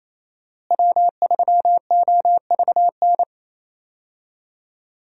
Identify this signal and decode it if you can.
Morse code, heard as a keyed tone, about 27 wpm, 700 Hz. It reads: W3OVN
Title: USA Call Signs - 5 Characters - Repeated 3x - 28wpm